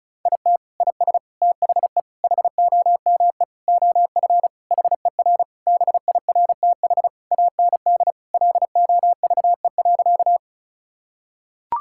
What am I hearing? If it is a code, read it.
Morse code, heard as a keyed tone, about 36 wpm, 700 Hz. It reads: IT IS THE HOME OF HER BIRTH AND LOVE.
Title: Sentences from Top 1000 Words - Review Part 5 - 35wpm